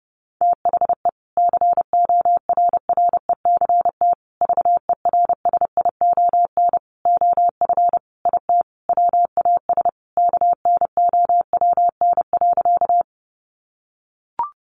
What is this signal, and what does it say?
Morse code, heard as a keyed tone, about 30 wpm, 700 Hz. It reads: THE CORRECT VERSION OF IT WAS KNOWN.
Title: Sentences from Top 1400 Words 30wpm